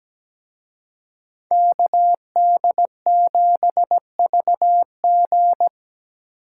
Morse code, heard as a keyed tone, about 20 wpm, 700 Hz. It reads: KD7VG
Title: Speed Racing - USA 2x2 Call Signs 17wpm